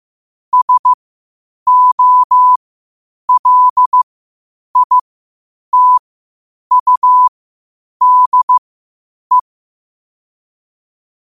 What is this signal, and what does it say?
Morse code, heard as a keyed tone, about 15 wpm, spaced as for 8 wpm, 1000 Hz. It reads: SOLITUDE
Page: qotd-at-8-wpm